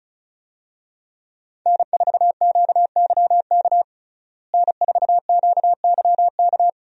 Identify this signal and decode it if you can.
Morse code, heard as a keyed tone, about 35 wpm, 700 Hz. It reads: N4QYK N4QYK
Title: USA Call Signs - 5 Characters - Repeated 2x - 35wpm